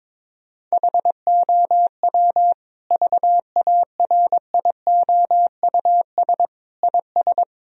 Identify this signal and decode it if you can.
Morse code, heard as a keyed tone, about 22 wpm, 700 Hz. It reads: HOW VARIOUS IS
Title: Sentences from Top 1000 Words - Review Part 4 - 22wpm